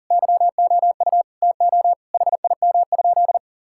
Morse code, heard as a keyed tone, about 37 wpm, 700 Hz. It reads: YOU TO HIM?
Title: Sentences from Top 1600 Words 40wpm